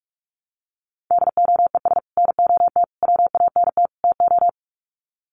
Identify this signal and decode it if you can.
Morse code, heard as a keyed tone, about 44 wpm, 700 Hz. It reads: DOES NOT WANT TO
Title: Sets of 4 Words 45wpm